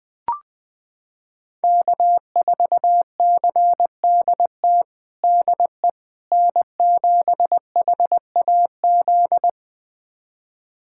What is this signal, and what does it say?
Morse code, heard as a keyed tone, about 20 wpm, 700 Hz. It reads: K4CDT DE N7HAZ
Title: USA Callsign DE Callsign - 5 Characters - 20wpm